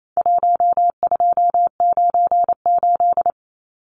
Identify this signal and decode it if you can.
Morse code, heard as a keyed tone, about 28 wpm, 700 Hz. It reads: 1298
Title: Numbers - 4-Digit 28wpm